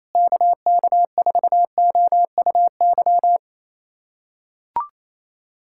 Morse code, heard as a keyed tone, about 28 wpm, 700 Hz. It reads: KK4OUY
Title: USA Callsign DE Callsign 28wpm